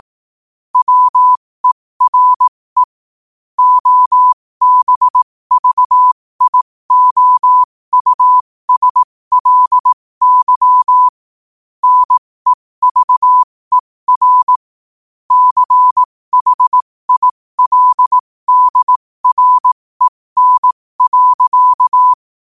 Morse code, heard as a keyed tone, about 18 wpm, spaced as for 15 wpm, 1000 Hz. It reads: WERE OBVIOUSLY NEVER CHILDREN.
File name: qotdfiles / 15wpm